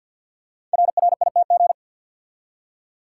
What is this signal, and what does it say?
Morse code, heard as a keyed tone, about 50 wpm, 700 Hz. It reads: RRITG
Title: Code Groups - 5 Letter:Numbers - 50wpm